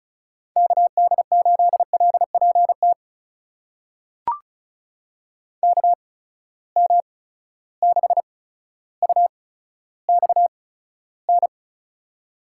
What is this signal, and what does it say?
Morse code, heard as a keyed tone, about 36 wpm, spaced as for 15 wpm, 700 Hz. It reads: KD8LPT KM6UXN
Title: USA Call Signs - 6 Characters - 8x Character Spacing - 35wpm